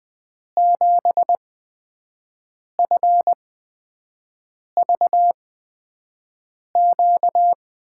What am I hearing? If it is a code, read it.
Morse code, heard as a keyed tone, about 20 wpm, spaced as for 5 wpm, 700 Hz. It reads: 7FVQ
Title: USA Call Signs - 8x Character Spacing - 20wpm